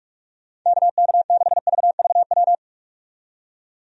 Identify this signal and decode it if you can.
Morse code, heard as a keyed tone, about 46 wpm, 700 Hz. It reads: KK6VVW
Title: USA Call Signs - 6 Characters - 45wpm